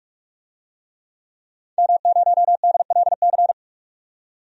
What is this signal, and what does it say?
Morse code, heard as a keyed tone, about 46 wpm, 700 Hz. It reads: M0DLC
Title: DXCC-100 Call Signs - 5 Characters - 45wpm